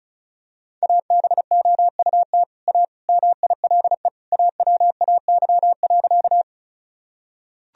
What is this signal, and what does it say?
Morse code, heard as a keyed tone, about 36 wpm, 700 Hz. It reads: ABOUT A MILE AWAY.
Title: Sentences from Top 1000 Words - Review Part 1 - 35wpm